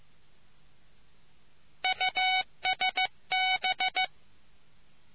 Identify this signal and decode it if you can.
Morse code, heard as a keyed tone, about 15 wpm, 750 Hz. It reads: USB